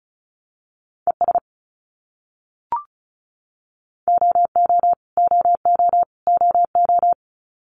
Morse code, heard as a keyed tone, about 34 wpm, 700 Hz. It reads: ES OO OO OO
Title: Abbreviations Repeated 3x - 35wpm